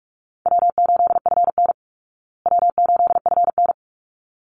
Morse code, heard as a keyed tone, about 44 wpm, 700 Hz. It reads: W8FN W8FN
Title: Easy Call Signs Repeated 2x - 45wpm